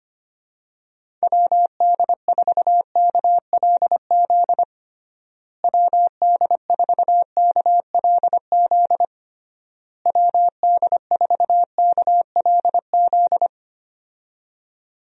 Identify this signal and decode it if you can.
Morse code, heard as a keyed tone, about 25 wpm, 700 Hz. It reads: WD4KLZ WD4KLZ WD4KLZ
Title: USA Call Signs - 6 Characters - Repeated 3x - 25wpm